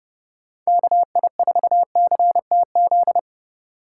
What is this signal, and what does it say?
Morse code, heard as a keyed tone, about 30 wpm, 700 Hz. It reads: KI4CTZ